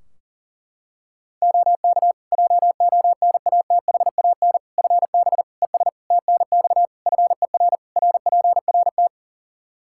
Morse code, heard as a keyed tone, about 37 wpm, 700 Hz. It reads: OK JONATHAN FB ES TNX FER RPRT
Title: Practice QSO - Exchange 2 - Info - 40wpm